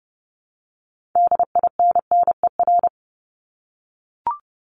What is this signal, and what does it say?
Morse code, heard as a keyed tone, about 30 wpm, 700 Hz. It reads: DINNER
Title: Top 1751-2000 Words 30wpm